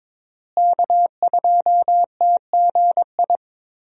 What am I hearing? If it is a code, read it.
Morse code, heard as a keyed tone, about 22 wpm, 700 Hz. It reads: K2TGI
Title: USA Call Signs - 5 Characters - Repeated 3x - 22wpm